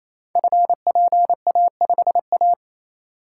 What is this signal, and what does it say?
Morse code, heard as a keyed tone, about 29 wpm, 700 Hz. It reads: FPA5A
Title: Code Groups - 5 Letter:Numbers - 28wpm